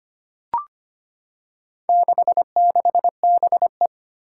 Morse code, heard as a keyed tone, about 25 wpm, 700 Hz. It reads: 66BE
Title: Four Challenging Letter-Number Combinations 25wpm